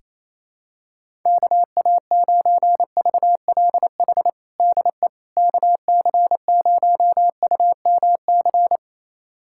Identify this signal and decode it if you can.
Morse code, heard as a keyed tone, about 28 wpm, 700 Hz. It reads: KA9VLH DE KC0UMC
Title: USA Callsign DE Callsign - 6 Characters - 28wpm